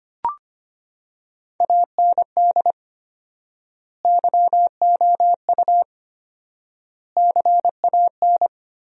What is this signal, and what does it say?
Morse code, heard as a keyed tone, about 25 wpm, 700 Hz. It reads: AND YOU CAN
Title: 4x Word-Spacing - Sentences from Top 500 Words - 25wpm